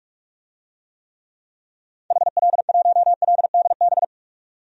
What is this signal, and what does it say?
Morse code, heard as a keyed tone, about 46 wpm, 700 Hz. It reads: HL1LDB
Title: DXCC-100 Call Signs - 6 Characters - 45wpm